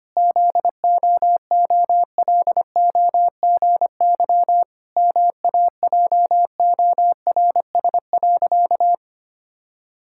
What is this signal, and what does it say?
Morse code, heard as a keyed tone, about 25 wpm, 700 Hz. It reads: ZOOLOGY MAJORS.
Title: Pangram Challenge v2 - 25wpm